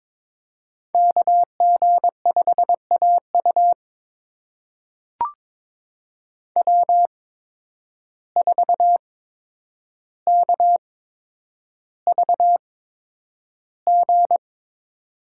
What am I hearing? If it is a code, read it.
Morse code, heard as a keyed tone, about 22 wpm, spaced as for 9 wpm, 700 Hz. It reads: KG5AU W4KVG
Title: USA Call Signs - 5 Characters - 8x Character Spacing - 22wpm